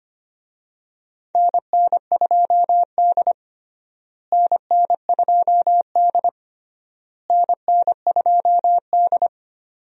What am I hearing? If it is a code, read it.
Morse code, heard as a keyed tone, about 25 wpm, 700 Hz. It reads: NN2D NN2D NN2D
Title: USA Call Signs - 4 Characters - Repeated 3x - 25wpm